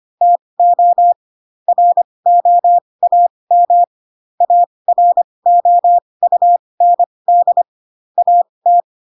Morse code, heard as a keyed tone, about 25 wpm, spaced as for 20 wpm, 700 Hz. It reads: TO ROAM AROUND AT
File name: Pinocchio / Pinocchio ch6 15 25 ik1hge